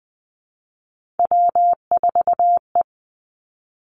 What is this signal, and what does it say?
Morse code, heard as a keyed tone, about 20 wpm, 700 Hz. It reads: W4E
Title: Three Challenging Letter-Number Combinations 20wpm